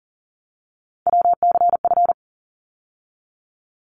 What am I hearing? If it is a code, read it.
Morse code, heard as a keyed tone, about 40 wpm, 700 Hz. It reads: WCF
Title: ARRL Sections 40wpm